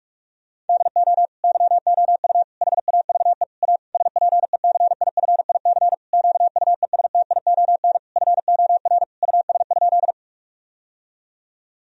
Morse code, heard as a keyed tone, about 46 wpm, 700 Hz. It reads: DO YOU HAVE A SPECIFIC QUESTION FOR US?
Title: Sentences from Top 1000 Words - Review Part 3 - 45wpm